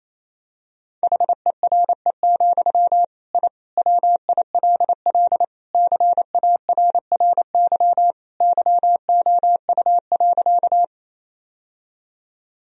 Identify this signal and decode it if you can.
Morse code, heard as a keyed tone, about 28 wpm, 700 Hz. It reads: HERE, I WILL CARRY YOU.
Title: Sentences from Top 1000 Words - Review Part 1 - 28wpm